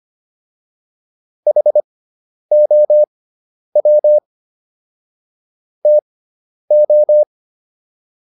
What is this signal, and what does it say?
Morse code, heard as a keyed tone, about 25 wpm, spaced as for 10 wpm, 600 Hz. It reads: HOW TO